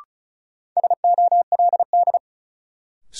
Morse code, heard as a keyed tone, about 36 wpm, 700 Hz. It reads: SOLD
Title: Sentences from Top 1400 Words - New Word Drill Alt - 35wpm